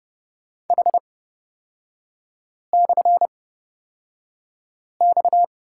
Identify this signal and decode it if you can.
Morse code, heard as a keyed tone, about 29 wpm, spaced as for 5 wpm, 700 Hz. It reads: H/X
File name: Lesson 37e - Letter x - Three Characters 30wpm